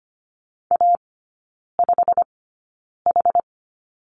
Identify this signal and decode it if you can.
Morse code, heard as a keyed tone, about 25 wpm, spaced as for 9 wpm, 690 Hz. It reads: A5H
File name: Lesson 15e - Number 5 - Three Characters 25wpm